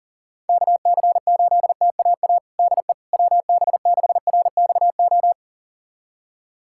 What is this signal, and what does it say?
Morse code, heard as a keyed tone, about 40 wpm, 700 Hz. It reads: KC8TAA DE WB6RXO
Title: USA Callsign DE Callsign - 6 Characters - 40wpm